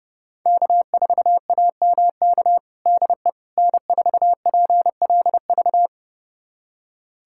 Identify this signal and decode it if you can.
Morse code, heard as a keyed tone, about 32 wpm, 700 Hz. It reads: K4AMK DE N4PLV
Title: USA Callsign DE Callsign - 5 Characters - 30wpm